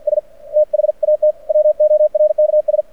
Morse code, heard as a keyed tone, about 52 wpm, 610 Hz. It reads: S THAT WORKS
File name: archived / CRSnet